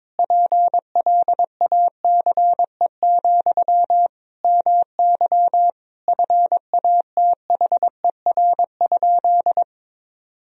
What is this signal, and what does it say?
Morse code, heard as a keyed tone, about 22 wpm, 700 Hz. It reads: PLACE, MY FATHER?
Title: Sentences from Top 400 Words and Abbreviations 22wpm